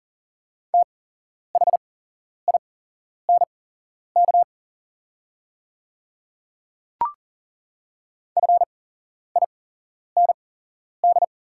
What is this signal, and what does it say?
Morse code, heard as a keyed tone, about 40 wpm, spaced as for 11 wpm, 690 Hz. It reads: THINK FIND
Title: 8x Character Spacing - Top 300 Words - 40wpm